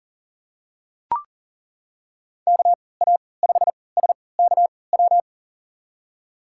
Morse code, heard as a keyed tone, about 40 wpm, spaced as for 23 wpm, 700 Hz. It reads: KA5SKW
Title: USA Call Signs - 6 Characters - 3x Character Spacing - 40wpm